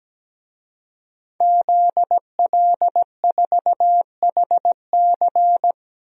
Morse code, heard as a keyed tone, about 17 wpm, 700 Hz. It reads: ZL4HC
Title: DXCC-100 Call Signs - 5 Characters - 17wpm